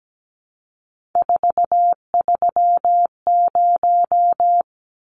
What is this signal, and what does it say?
Morse code, heard as a keyed tone, about 17 wpm, 700 Hz. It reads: 430
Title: Numbers - 3-Digit 17wpm